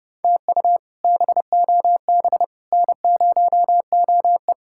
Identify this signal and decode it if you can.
Morse code, heard as a keyed tone, about 30 wpm, 700 Hz. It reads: TU BOB N0OE
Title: K1USN SST QSO Practice - 30wpm